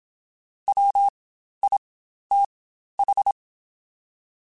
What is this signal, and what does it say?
Morse code, heard as a keyed tone, about 27 wpm, spaced as for 12 wpm, 800 Hz. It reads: WITH
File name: Spaced Characters / Spaced Characters-mp3s / Weather 3 17 sp @25